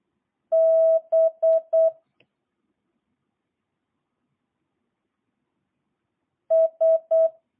Morse code, heard as a keyed tone, about 8 wpm, 650 Hz. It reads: B S